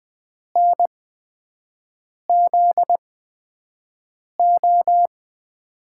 Morse code, heard as a keyed tone, about 21 wpm, spaced as for 5 wpm, 700 Hz. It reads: NZO
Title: USA Call Signs - 8x Character Spacing - 20wpm